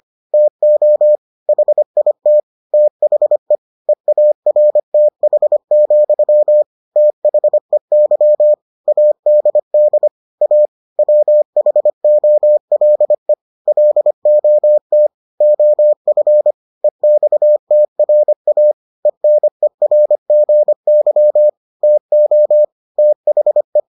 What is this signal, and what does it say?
Morse code, heard as a keyed tone, about 25 wpm, 600 Hz. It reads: TO HIT THE EARTH, THEY ADD A WHOLE LOT OF EXTRA ENERGY TO THE